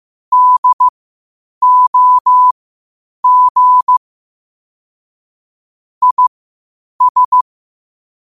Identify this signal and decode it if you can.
Morse code, heard as a keyed tone, about 15 wpm, spaced as for 8 wpm, 1000 Hz. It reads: DOG IS
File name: qotdfiles / 8wpm